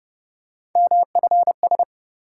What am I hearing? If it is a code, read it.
Morse code, heard as a keyed tone, about 28 wpm, 700 Hz. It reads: MFS